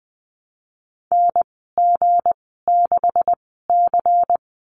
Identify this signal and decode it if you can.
Morse code, heard as a keyed tone, about 20 wpm, spaced as for 15 wpm, 700 Hz. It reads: NG6C
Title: 173 Lesson 28g - Number 9 - Call Signs 20wpm